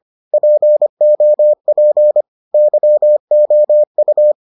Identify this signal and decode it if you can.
Morse code, heard as a keyed tone, about 25 wpm, 600 Hz. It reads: POP YOU